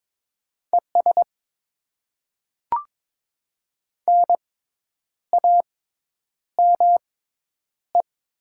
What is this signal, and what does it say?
Morse code, heard as a keyed tone, about 22 wpm, spaced as for 9 wpm, 700 Hz. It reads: ES NAME
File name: Top 20 QSO Elements - 6x Character Spacing - 22wpm